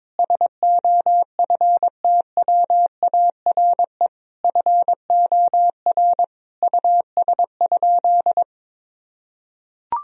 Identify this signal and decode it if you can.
Morse code, heard as a keyed tone, about 22 wpm, 700 Hz. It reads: SOFTWARE FOR US?
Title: Sentences from Top 1000 Words - Review Part 3 - 22wpm